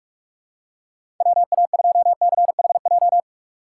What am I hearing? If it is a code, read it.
Morse code, heard as a keyed tone, about 47 wpm, 700 Hz. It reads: WA2CHJ
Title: USA Call Signs - 6 Characters - 4x Character Spacing - 45wpm